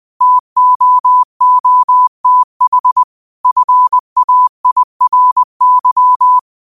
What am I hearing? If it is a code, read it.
Morse code, heard as a keyed tone, about 20 wpm, 1000 Hz. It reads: TOOTH FAIRY
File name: qotdfiles / 20wpm